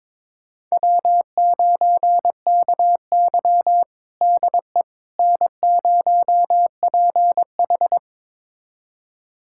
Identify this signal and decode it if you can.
Morse code, heard as a keyed tone, about 22 wpm, 700 Hz. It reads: W9KY DE N0PH
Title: USA Callsign DE Callsign - 4 Characters - 22wpm